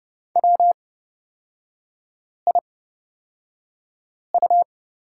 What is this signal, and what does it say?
Morse code, heard as a keyed tone, about 29 wpm, spaced as for 5 wpm, 700 Hz. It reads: WIU
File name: Lesson 36e - BT - Three Characters 30wpm